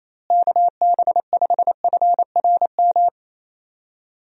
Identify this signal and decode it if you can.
Morse code, heard as a keyed tone, about 28 wpm, 700 Hz. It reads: KB5FRM
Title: USA Call Signs - 6 Characters - Repeated 2x - 28wpm